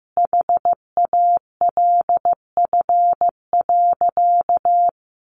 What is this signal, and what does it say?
Morse code, heard as a keyed tone, about 15 wpm, 700 Hz. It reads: HALF.
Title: Sentences from Top 1100 Words 15wpm